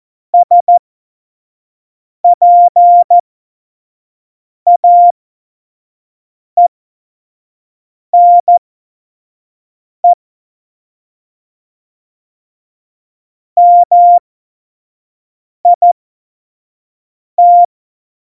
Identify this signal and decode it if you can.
Morse code, heard as a keyed tone, about 14 wpm, spaced as for 5 wpm, 700 Hz. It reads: SPAENE MIT
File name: Morsekurs / Witze / Witz 14. Januar 2020, 25 F70